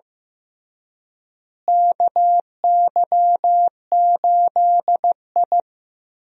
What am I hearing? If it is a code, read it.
Morse code, heard as a keyed tone, about 15 wpm, 700 Hz. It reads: KY8I
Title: USA Call Signs - 4 Characters - Repeated 3x - 15wpm